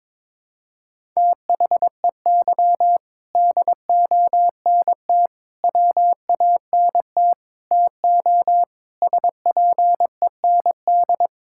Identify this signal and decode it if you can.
Morse code, heard as a keyed tone, about 22 wpm, 700 Hz. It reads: THEY DONT WANT TO SPEND
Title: Sentences from Top 1000 Words - Review Part 5 - 22wpm